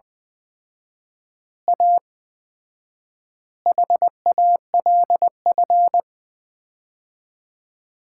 Morse code, heard as a keyed tone, about 21 wpm, 700 Hz. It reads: A HALF
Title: 4x Word-Spacing - Sets of 5 Words - 20wpm